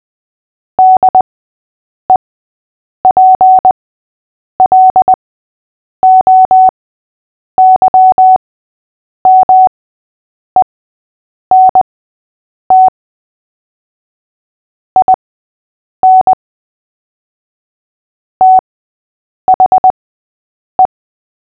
Morse code, heard as a keyed tone, about 20 wpm, spaced as for 8 wpm, 750 Hz. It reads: DEPLOYMENT IN THE
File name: cw / fists / Short-Story-3-8wpm